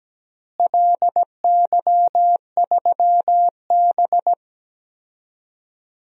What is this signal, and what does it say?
Morse code, heard as a keyed tone, about 17 wpm, 700 Hz. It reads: LY3B
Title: DXCC-100 Call Signs - 4 Characters - Repeated 3x - 17wpm